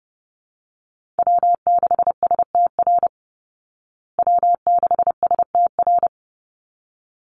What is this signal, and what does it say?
Morse code, heard as a keyed tone, about 28 wpm, 700 Hz. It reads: W6STR W6STR
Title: Call Signs Repeated 3x - 30wpm